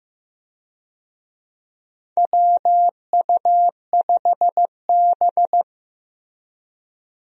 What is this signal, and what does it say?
Morse code, heard as a keyed tone, about 15 wpm, 700 Hz. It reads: WU5B
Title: USA Call Signs - 4 Characters - Repeated 2x - 15wpm